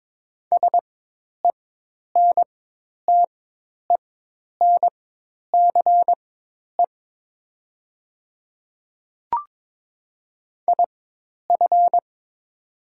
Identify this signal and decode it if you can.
Morse code, heard as a keyed tone, about 22 wpm, spaced as for 10 wpm, 700 Hz. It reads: SENTENCE IF
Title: 4x Character Spacing - Top 300 Words - 22wpm